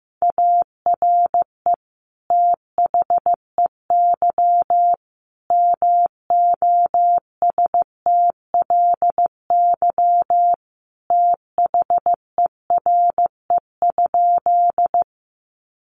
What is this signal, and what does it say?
Morse code, heard as a keyed tone, about 15 wpm, 700 Hz. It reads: ARE THEY MOSTLY THERE?
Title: Sentences from Top 1300 Words 15wpm